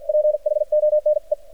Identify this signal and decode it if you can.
Morse code, heard as a keyed tone, about 53 wpm, spaced as for 45 wpm, 600 Hz. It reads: PHONE
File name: archived / CRSnet